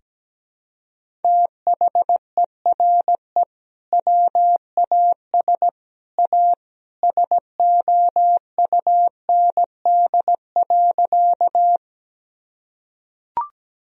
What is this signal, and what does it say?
Morse code, heard as a keyed tone, about 17 wpm, 700 Hz. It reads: THERE WAS A SOUND.
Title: Sentences from Top 1000 Words - Review Part 1 - 17wpm